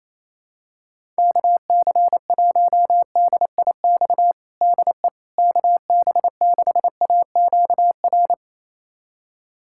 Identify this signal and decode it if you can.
Morse code, heard as a keyed tone, about 28 wpm, 700 Hz. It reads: KC1DIX DE KB6AQR
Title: USA Callsign DE Callsign 28wpm